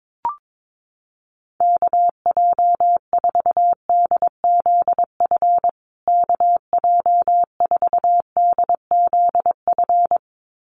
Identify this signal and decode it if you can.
Morse code, heard as a keyed tone, about 22 wpm, 700 Hz. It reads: KJ4DZF KJ4DZF
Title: USA Call Signs Repeated 2x 22wpm